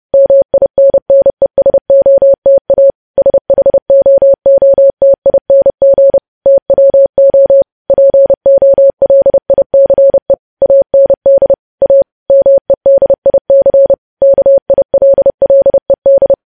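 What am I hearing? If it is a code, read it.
Morse code, heard as a keyed tone, about 30 wpm, 550 Hz. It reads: MINNESOTA SHOOTING TWO POLICE AND A MEDIC KILLED
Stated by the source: Morse news headlines sent at 30.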